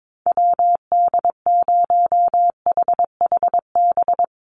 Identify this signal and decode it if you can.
Morse code, heard as a keyed tone, about 22 wpm, 700 Hz. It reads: WD0HHB
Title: USA Call Signs 22wpm